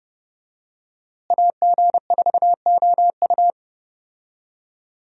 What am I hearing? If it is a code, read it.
Morse code, heard as a keyed tone, about 30 wpm, 700 Hz. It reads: AG4OU
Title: USA Call Signs - 5 Characters - Repeated 3x - 30wpm